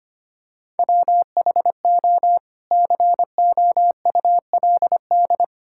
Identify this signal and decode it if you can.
Morse code, heard as a keyed tone, about 25 wpm, 700 Hz. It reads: WHO COULD
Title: Sentences from Top 1600 Words 25wpm